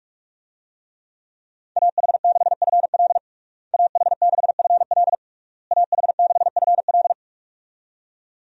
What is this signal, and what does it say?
Morse code, heard as a keyed tone, about 41 wpm, 700 Hz. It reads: AH6FL AH6FL AH6FL
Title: USA Call Signs - 5 Characters - Repeated 3x - 45wpm